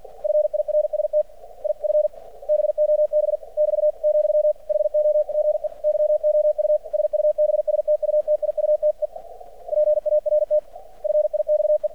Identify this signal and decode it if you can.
Morse code, heard as a keyed tone, about 50 wpm, 600 Hz. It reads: FIRST IF BOB <BT> , HOGE YOU SUBSTUTIUTE PART FIXE
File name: archived / CRSnet